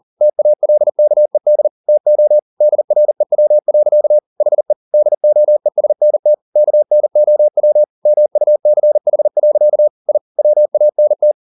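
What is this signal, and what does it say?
Morse code, heard as a keyed tone, about 40 wpm, 600 Hz. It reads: TALKED TO DREW. HE DOESNT KNOW MUCH. I WANT
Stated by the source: Morse code at 40 wpm, practice text, CW med Gnister.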